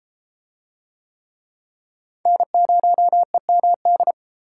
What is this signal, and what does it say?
Morse code, heard as a keyed tone, about 33 wpm, 700 Hz. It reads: N0EMD